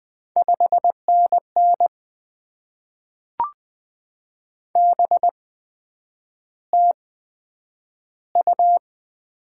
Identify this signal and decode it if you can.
Morse code, heard as a keyed tone, about 20 wpm, spaced as for 9 wpm, 700 Hz. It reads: 5NN BTU
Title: Top 20 QSO Elements - 8x Character Spacing - 20wpm